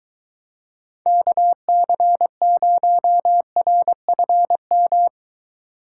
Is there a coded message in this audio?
Yes: Morse code from 1.1 to 5.1 s, heard as a keyed tone, about 23 wpm, 700 Hz. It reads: KC0RFM